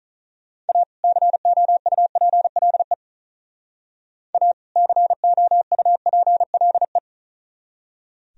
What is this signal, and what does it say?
Morse code, heard as a keyed tone, about 41 wpm, 700 Hz. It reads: A COUPLE A COUPLE
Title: Speed-Racing - Sets of 2 Words 35wpm